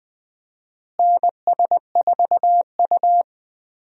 Morse code, heard as a keyed tone, about 20 wpm, 700 Hz. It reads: NS4U
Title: USA Call Signs - 4 Characters - Repeated 3x - 20wpm